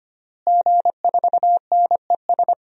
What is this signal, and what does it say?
Morse code, heard as a keyed tone, about 25 wpm, 700 Hz. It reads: G4NES
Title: DXCC-100 Call Signs - 5 Characters - Repeated 2x - 25wpm